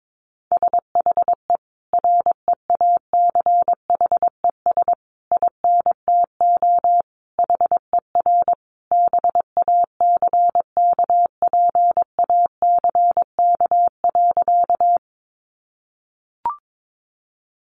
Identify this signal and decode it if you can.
Morse code, heard as a keyed tone, about 22 wpm, 700 Hz. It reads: SHE REACHES INTO HER BACKPACK.
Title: Additional Easy Phrases 22wpm